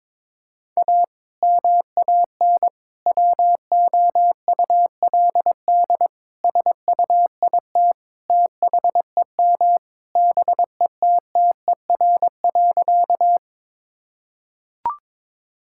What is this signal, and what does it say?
Morse code, heard as a keyed tone, about 22 wpm, 700 Hz. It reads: A MAN WOULD SUIT THEM BETTER.
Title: Sentences from Top 1600 Words 22wpm